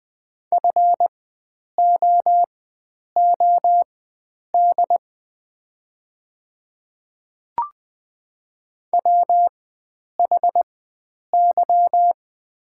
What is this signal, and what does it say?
Morse code, heard as a keyed tone, about 20 wpm, spaced as for 9 wpm, 700 Hz. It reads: FOOD WHY
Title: 4x Character Spacing - Top 300 Words - 20wpm